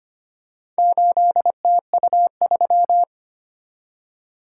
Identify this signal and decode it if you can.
Morse code, heard as a keyed tone, about 25 wpm, 700 Hz. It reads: 8TU3
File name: practice / 25wpm / Four Letter-Number Combinations - 25wpm